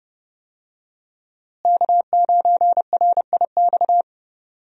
Morse code, heard as a keyed tone, about 32 wpm, 700 Hz. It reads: K9RIX
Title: USA Call Signs - 5 Characters - Repeated 2x - 30wpm